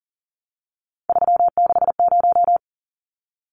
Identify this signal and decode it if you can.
Morse code, heard as a keyed tone, about 38 wpm, 700 Hz. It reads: 360
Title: Numbers - 3-Digit 40wpm